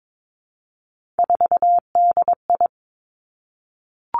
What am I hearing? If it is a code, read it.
Morse code, heard as a keyed tone, about 22 wpm, 700 Hz. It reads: 4DI
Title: Three Challenging Letter-Number Combinations 22wpm